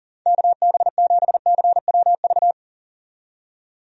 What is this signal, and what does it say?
Morse code, heard as a keyed tone, about 38 wpm, 700 Hz. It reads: KB7CWV
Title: USA Call Signs - 6 Characters - 40wpm